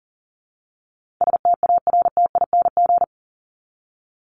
Morse code, heard as a keyed tone, about 40 wpm, 700 Hz. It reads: STARTING
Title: ING Suffix - Top 100 Words with Base Words 40wpm